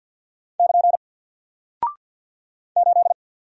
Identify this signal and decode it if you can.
Morse code, heard as a keyed tone, about 51 wpm, 700 Hz. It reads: <KN> 8
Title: Single Character (Letter:Number:Punctuation:Prosign) Alt 50wpm